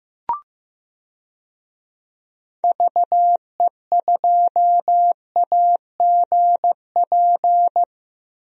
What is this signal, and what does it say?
Morse code, heard as a keyed tone, about 15 wpm, 700 Hz. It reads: VE2AGP
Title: DXCC-100 Call Signs - 6 Characters - Repeated 2x - 15wpm